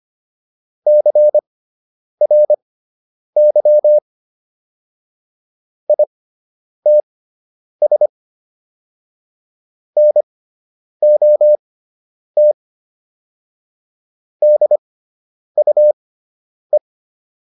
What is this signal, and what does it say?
Morse code, heard as a keyed tone, about 25 wpm, spaced as for 9 wpm, 600 Hz. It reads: CRY ITS NOT DUE